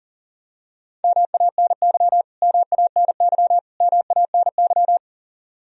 Morse code, heard as a keyed tone, about 40 wpm, 700 Hz. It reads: MANY MANY MANY
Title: Top 100 Words Repeated 40wpm